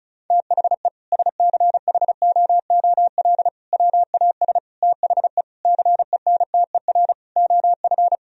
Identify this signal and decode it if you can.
Morse code, heard as a keyed tone, about 36 wpm, 700 Hz. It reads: THE SCHOOL WAS THE CENTER OF
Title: Sentences from Top 1000 Words - Review Part 2 - 35wpm